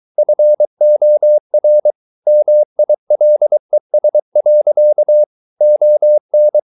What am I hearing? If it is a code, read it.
Morse code, heard as a keyed tone, about 23 wpm, 600 Hz. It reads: FOR MILES. ON